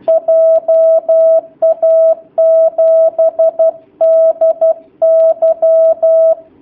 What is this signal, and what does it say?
Morse code, heard as a keyed tone, about 12 wpm, 650 Hz. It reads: JA7DY
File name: jh8cbh / 2019cw / sp11- w